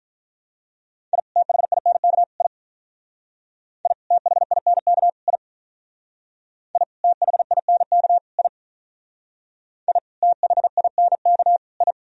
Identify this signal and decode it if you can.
Morse code, heard as a keyed tone, about 46 wpm, 700 Hz. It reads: I THINK I I THINK I I THINK I I THINK I